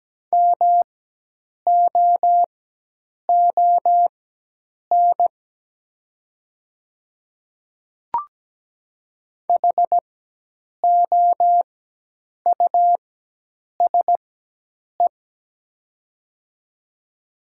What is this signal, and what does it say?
Morse code, heard as a keyed tone, about 17 wpm, spaced as for 8 wpm, 700 Hz. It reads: MOON HOUSE
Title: 4x Character Spacing - Top 500 Words - 17wpm